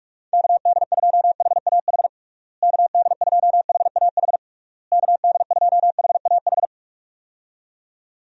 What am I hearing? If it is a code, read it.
Morse code, heard as a keyed tone, about 41 wpm, 700 Hz. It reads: KD2HAH KD2HAH KD2HAH
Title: USA Call Signs - 6 Characters - Repeated 3x - 45wpm